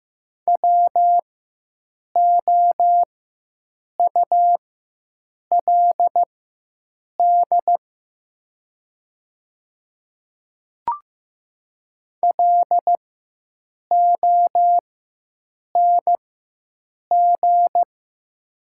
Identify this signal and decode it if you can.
Morse code, heard as a keyed tone, about 15 wpm, spaced as for 7 wpm, 700 Hz. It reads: WOULD LONG
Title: 4x Character Spacing - Top 100 Words - 15wpm